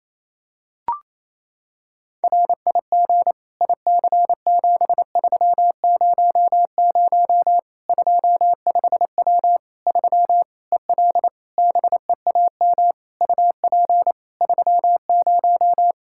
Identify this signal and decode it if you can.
Morse code, heard as a keyed tone, about 28 wpm, 700 Hz. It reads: RIG IC7300 25W 3 EL BEAM UP 30
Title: Practice QSO - Exchange 2 - Info - 28wpm